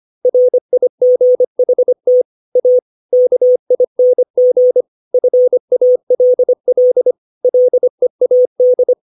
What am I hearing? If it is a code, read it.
Morse code, heard as a keyed tone, about 25 wpm, 500 Hz. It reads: RIGHT A KING FALL LEAD
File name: 632 words / 632 Words 25WPM